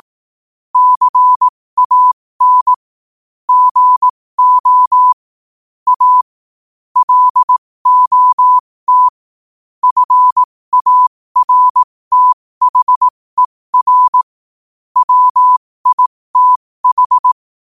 Morse code, heard as a keyed tone, about 18 wpm, spaced as for 15 wpm, 1000 Hz. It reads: CAN GO A LOT FARTHER WITH
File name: qotdfiles / 15wpm